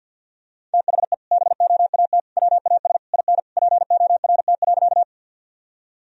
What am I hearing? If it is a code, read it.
Morse code, heard as a keyed tone, about 50 wpm, 700 Hz. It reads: THE BOAT WAS IN PORT.
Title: Sentences from Top 1000 Words - Review Part 1 - 50wpm